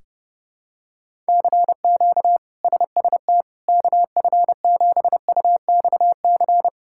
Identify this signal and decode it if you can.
Morse code, heard as a keyed tone, about 31 wpm, 700 Hz. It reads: CQ SST KF7UXC
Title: K1USN SST QSO Practice - 30wpm